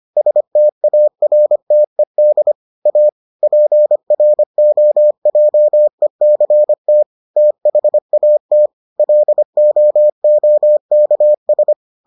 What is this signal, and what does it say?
Morse code, heard as a keyed tone, about 25 wpm, 600 Hz. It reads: STARTED A PROJECT THAT LOOKS